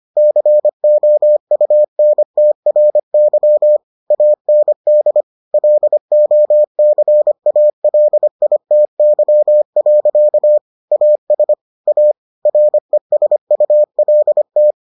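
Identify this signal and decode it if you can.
Morse code, heard as a keyed tone, about 25 wpm, 600 Hz. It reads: COUNTRY AND LOCALITY. AS A RESULT
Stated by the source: Morse edition of the Foundations of Amateur Radio episodes.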